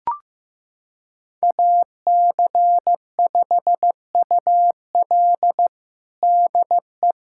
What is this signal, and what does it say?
Morse code, heard as a keyed tone, about 15 wpm, 700 Hz. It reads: AC5UL DE
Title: USA Callsign DE Callsign 15wpm